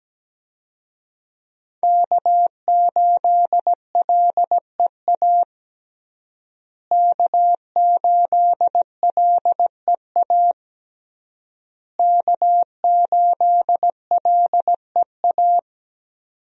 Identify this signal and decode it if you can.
Morse code, heard as a keyed tone, about 17 wpm, 700 Hz. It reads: K8LEA K8LEA K8LEA
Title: USA Call Signs - 5 Characters - Repeated 3x - 17wpm